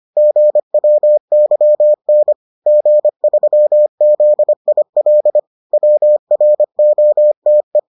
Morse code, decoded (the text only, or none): GWYN G3ZIL WROTE